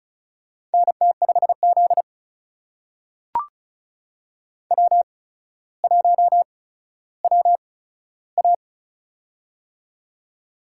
Morse code, heard as a keyed tone, about 34 wpm, spaced as for 15 wpm, 700 Hz. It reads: NT5Z W1WA
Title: USA Call Signs - 4 Characters - 8x Character Spacing - 35wpm